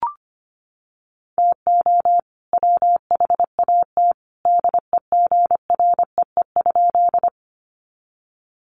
Morse code, heard as a keyed tone, about 25 wpm, 700 Hz. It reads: TO WHAT DEGREE?
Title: Sentences from Top 1500 Words 25wpm